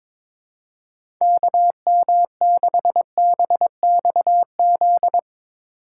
Morse code, heard as a keyed tone, about 22 wpm, 700 Hz. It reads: KM6BXZ
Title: USA Call Signs - 6 Characters - 2x Character Spacing - 22wpm